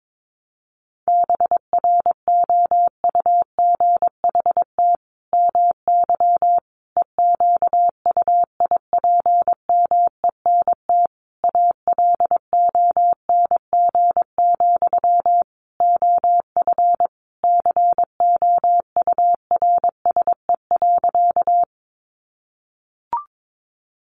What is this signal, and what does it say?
Morse code, heard as a keyed tone, about 22 wpm, 700 Hz. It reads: BROUGHT MY EQUIPMENT ALONG, OF COURSE.
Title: Sentences from Top 700 Words 22wpm